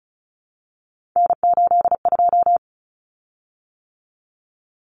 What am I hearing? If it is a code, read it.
Morse code, heard as a keyed tone, about 34 wpm, 700 Hz. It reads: N82
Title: Common Cut Numbers - 3-Digit 35wpm